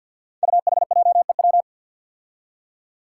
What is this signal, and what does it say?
Morse code, heard as a keyed tone, about 52 wpm, 700 Hz. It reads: UHJEW
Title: Five Challenging Letter-Number Combinations 50wpm